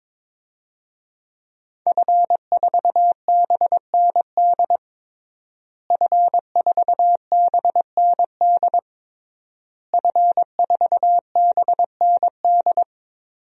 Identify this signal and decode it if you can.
Morse code, heard as a keyed tone, about 22 wpm, 700 Hz. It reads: F4BND F4BND F4BND
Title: DXCC-100 Call Signs - 5 Characters - Repeated 3x - 22wpm